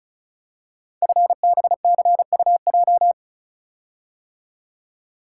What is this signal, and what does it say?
Morse code, heard as a keyed tone, about 36 wpm, 700 Hz. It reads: FBCUJ
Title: Code Groups - 5 Letters - 35wpm